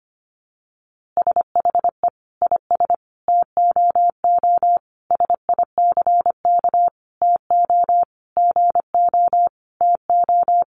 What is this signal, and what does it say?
Morse code, heard as a keyed tone, about 25 wpm, 700 Hz. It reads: SHE IS TOO SICK TO GO TO
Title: Easy Phrases 25wpm